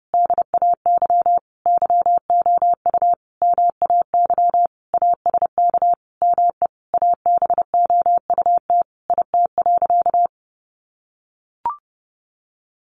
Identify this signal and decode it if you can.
Morse code, heard as a keyed tone, about 30 wpm, 700 Hz. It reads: DAY YOU MAY ASK ME ABOUT IT.
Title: Sentences from Top 200 Words 30wpm